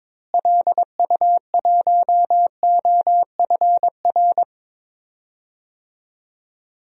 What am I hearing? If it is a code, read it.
Morse code, heard as a keyed tone, about 22 wpm, 700 Hz. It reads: LU1OFR
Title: DXCC-100 Call Signs - 6 Characters - Repeated 3x - 22wpm